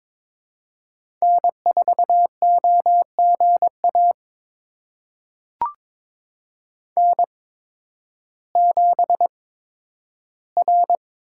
Morse code, heard as a keyed tone, about 23 wpm, spaced as for 12 wpm, 700 Hz. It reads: N4OGA N7R
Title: USA Call Signs - 5 Characters - 8x Character Spacing - 22wpm